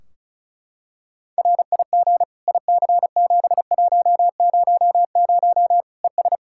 Morse code, heard as a keyed tone, about 36 wpm, 700 Hz. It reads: RIG IC7100 ES
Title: Practice QSO - Exchange 2 - Info - 35wpm